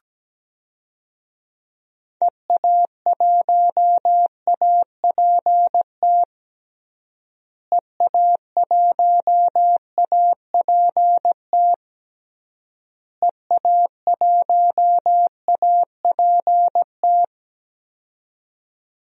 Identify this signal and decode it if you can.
Morse code, heard as a keyed tone, about 17 wpm, 700 Hz. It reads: EA1APT EA1APT EA1APT
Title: DXCC-100 Call Signs - 6 Characters - Repeated 3x - 17wpm